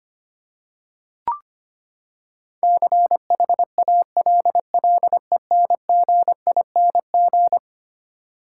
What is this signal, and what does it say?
Morse code, heard as a keyed tone, about 25 wpm, 700 Hz. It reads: CHALLENGING
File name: practice / 25wpm / Eleven Letter Words - 25wpm